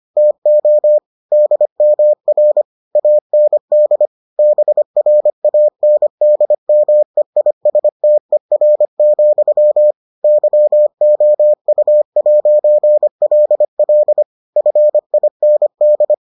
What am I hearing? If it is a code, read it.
Morse code, heard as a keyed tone, about 25 wpm, 600 Hz. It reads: TO DMR AND BRANDMEISTER, YOU'LL FIND